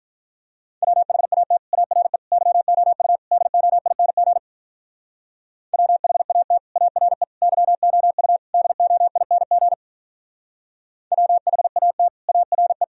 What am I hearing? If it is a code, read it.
Morse code, heard as a keyed tone, about 45 wpm, 700 Hz. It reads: WHAT ARE YOU DOING WHAT ARE YOU DOING WHAT ARE